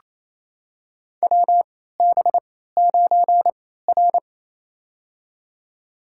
Morse code, heard as a keyed tone, about 28 wpm, spaced as for 16 wpm, 700 Hz. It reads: WB9R
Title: USA Call Signs - 3x Character Spacing - 28wpm